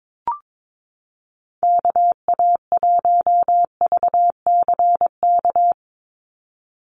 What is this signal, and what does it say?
Morse code, heard as a keyed tone, about 22 wpm, 700 Hz. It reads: KA1VCK
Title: USA Call Signs 22wpm